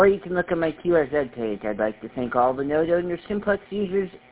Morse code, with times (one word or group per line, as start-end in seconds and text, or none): none